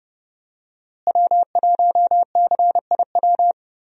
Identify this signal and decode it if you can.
Morse code, heard as a keyed tone, about 30 wpm, 700 Hz. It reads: W1CIW
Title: USA Call Signs - 5 Characters - Repeated 2x - 30wpm